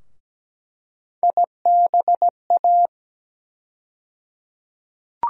Morse code, heard as a keyed tone, about 17 wpm, 700 Hz. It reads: IBA
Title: USA Call Sign Postfixes - 3 Letters 17wpm